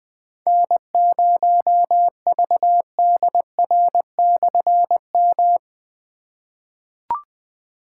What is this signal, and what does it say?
Morse code, heard as a keyed tone, about 20 wpm, 700 Hz. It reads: N0VDR/M
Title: Difficult Call Signs 20wpm